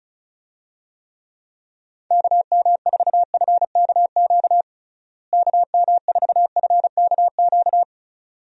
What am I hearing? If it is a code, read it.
Morse code, heard as a keyed tone, about 36 wpm, 700 Hz. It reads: KM4FKQ KM4FKQ
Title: USA Call Signs - 6 Characters - Repeated 2x - 35wpm